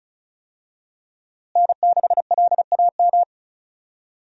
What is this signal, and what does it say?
Morse code, heard as a keyed tone, about 36 wpm, 700 Hz. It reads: N6LAM
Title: USA Call Signs - 5 Characters - Repeated 3x - 35wpm